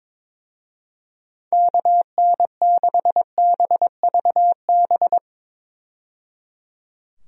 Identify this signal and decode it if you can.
Morse code, heard as a keyed tone, about 22 wpm, 700 Hz. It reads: KN6BVB
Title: USA Call Signs - 6 Characters - 22wpm